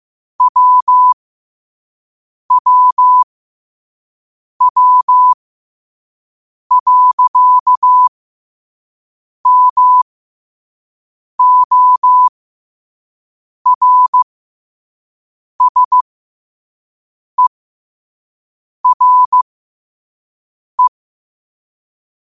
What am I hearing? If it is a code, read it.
Morse code, heard as a keyed tone, about 15 wpm, spaced as for 5 wpm, 1000 Hz. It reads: WWW.MORSERE
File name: 5wpm